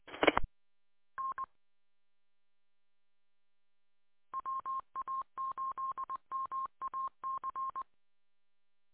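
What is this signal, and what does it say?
Morse code, heard as a keyed tone, about 20 wpm, 1060 Hz. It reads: I N WA8MAC
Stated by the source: Morse code recorded from the wa8mac-147200 audio archive.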